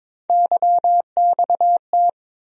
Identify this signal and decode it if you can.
Morse code, heard as a keyed tone, about 22 wpm, 700 Hz. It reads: YXT
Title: USA Call Sign Postfixes - 3 Letters 22wpm